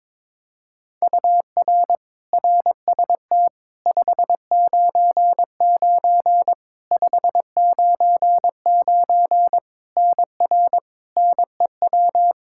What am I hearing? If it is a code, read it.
Morse code, heard as a keyed tone, about 22 wpm, 700 Hz. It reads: UR RST 599 599 NR NEW
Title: Practice QSO - Exchange 1 - Beginning - 22wpm